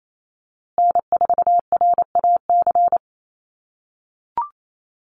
Morse code, heard as a keyed tone, about 28 wpm, 700 Hz. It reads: N4RAC
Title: USA Call Signs Repeated 2x 28wpm